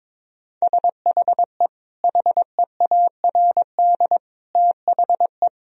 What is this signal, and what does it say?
Morse code, heard as a keyed tone, about 22 wpm, 700 Hz. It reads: SHE HEARD THE
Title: Sentences from Top 1000 Words - Review Part 1 - 22wpm